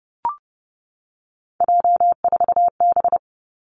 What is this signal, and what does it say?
Morse code, heard as a keyed tone, about 30 wpm, 700 Hz. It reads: J4B
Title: Three Challenging Letter-Number Combinations 30wpm